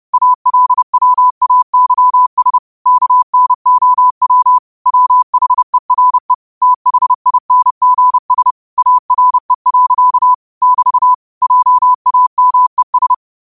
Morse code, heard as a keyed tone, about 30 wpm, 1000 Hz. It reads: ALWAYS KNOW WHERE THINGS ARE. <BT> JAMES